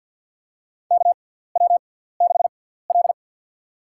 Morse code, heard as a keyed tone, about 53 wpm, spaced as for 17 wpm, 700 Hz. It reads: KW6L
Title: USA Call Signs - 4 Characters - 6x Character Spacing - 50wpm